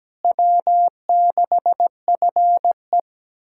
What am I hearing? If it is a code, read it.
Morse code, heard as a keyed tone, about 17 wpm, 700 Hz. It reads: W6FE
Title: USA Callsign DE Callsign - 4 Characters - 17wpm